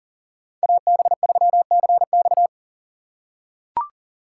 Morse code, heard as a keyed tone, about 40 wpm, 700 Hz. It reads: AB3CX
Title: USA Call Signs - 5 Characters - 4x Character Spacing - 40wpm